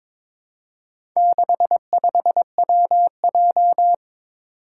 Morse code, heard as a keyed tone, about 22 wpm, 700 Hz. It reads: 65WJ
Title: Four Challenging Letter-Number Combinations 22wpm